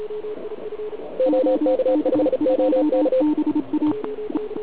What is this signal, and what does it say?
Morse code, heard as a keyed tone, about 36 wpm, 520 Hz, with noise. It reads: OK5OK
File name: ARCHIV / KX3 / OK1IF CUC175